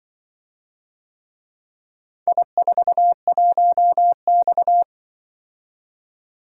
Morse code, heard as a keyed tone, about 25 wpm, 700 Hz. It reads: I41X
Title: Speed-Racing - Four Letter-Number Combinations - 20wpm